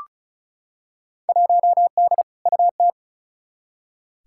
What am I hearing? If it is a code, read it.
Morse code, heard as a keyed tone, about 36 wpm, 700 Hz. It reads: 1D UT
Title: Field Day Exchanges - 35wpm